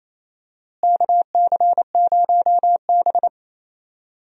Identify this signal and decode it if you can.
Morse code, heard as a keyed tone, about 28 wpm, 700 Hz. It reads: KC0B
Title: USA Call Signs - 4 Characters - Repeated 2x - 28wpm